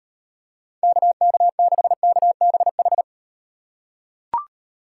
Morse code, heard as a keyed tone, about 37 wpm, 700 Hz. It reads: KK6KBH